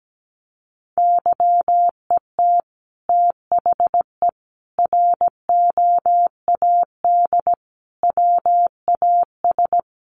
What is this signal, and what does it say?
Morse code, heard as a keyed tone, about 17 wpm, 700 Hz. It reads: YET THE ROAD WAS
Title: Sentences from Top 1400 Words 17wpm